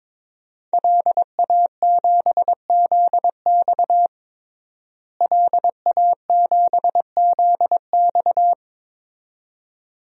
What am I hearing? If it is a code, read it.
Morse code, heard as a keyed tone, about 22 wpm, 700 Hz. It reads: LA7ZX LA7ZX
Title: DXCC-100 Call Signs - 5 Characters - Repeated 2x - 22wpm